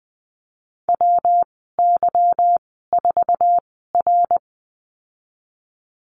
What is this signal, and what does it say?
Morse code, heard as a keyed tone, about 19 wpm, spaced as for 14 wpm, 700 Hz. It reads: WY4R
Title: 173 Lesson 28g - Number 9 - Call Signs 20wpm